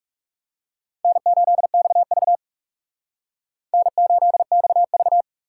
Morse code, heard as a keyed tone, about 39 wpm, 700 Hz. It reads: N8XV N8XV